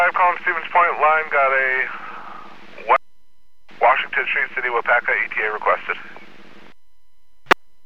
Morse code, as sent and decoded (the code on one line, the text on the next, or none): none